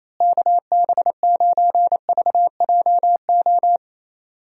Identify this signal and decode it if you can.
Morse code, heard as a keyed tone, about 27 wpm, 700 Hz. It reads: KB9VJO
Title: USA Call Signs - 6 Characters - Repeated 3x - 28wpm